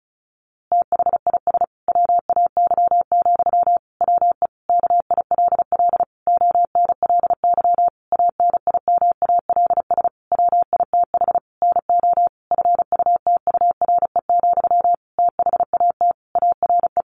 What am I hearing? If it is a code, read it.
Morse code, heard as a keyed tone, about 34 wpm, 700 Hz. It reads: THIS WAY, WE KILL ONLY ANIMALS WITH NO FUTURE, THAT ARE
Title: A Sound Of Thunder 35wpm